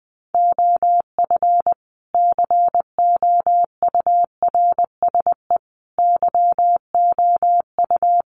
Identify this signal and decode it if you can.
Morse code, heard as a keyed tone, about 20 wpm, 700 Hz. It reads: OF COURSE YOU
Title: Sentences from Top 1300 Words 20wpm